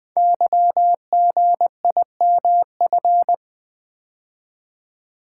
Morse code, heard as a keyed tone, about 20 wpm, 700 Hz. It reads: YGIMF
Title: Code Groups - 5 Letter:Numbers - 20wpm